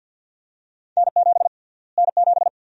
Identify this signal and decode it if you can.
Morse code, heard as a keyed tone, about 52 wpm, 700 Hz. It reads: N7 N7
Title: USA Call Sign Prefixes - Repeated 2x - 50wpm